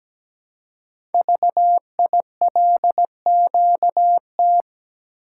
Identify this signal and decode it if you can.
Morse code, heard as a keyed tone, about 17 wpm, 700 Hz. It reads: VILQT
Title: Code Groups - 5 Letters - 17wpm